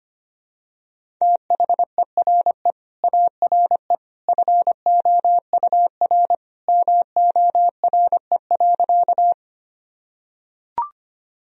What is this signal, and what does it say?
Morse code, heard as a keyed tone, about 25 wpm, 700 Hz. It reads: THERE ARE FOUR MORE.
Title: Sentences from Top 1000 Words - Review Part 1 - 25wpm